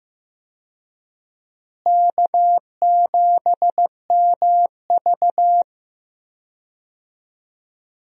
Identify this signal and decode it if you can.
Morse code, heard as a keyed tone, about 15 wpm, 700 Hz. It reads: K7MV
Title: USA Call Signs - 4 Characters - Repeated 2x - 15wpm